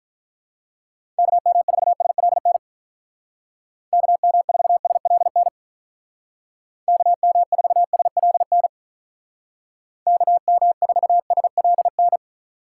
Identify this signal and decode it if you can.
Morse code, heard as a keyed tone, about 47 wpm, 700 Hz. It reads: KM4SLN KM4SLN KM4SLN KM4SLN